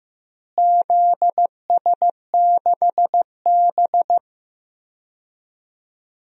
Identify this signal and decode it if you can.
Morse code, heard as a keyed tone, about 15 wpm, 700 Hz. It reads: ZS6B
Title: DXCC-100 Call Signs - 4 Characters - Repeated 3x - 15wpm